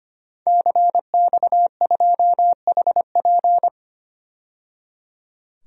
Morse code, heard as a keyed tone, about 25 wpm, 700 Hz. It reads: CX2HP
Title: DXCC-100 Call Signs - 5 Characters - 25wpm